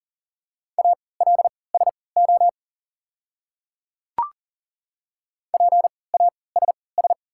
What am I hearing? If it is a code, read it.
Morse code, heard as a keyed tone, about 40 wpm, spaced as for 23 wpm, 700 Hz. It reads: ALSO PASS
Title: 3x Character Spacing - Top 500 Words - 40wpm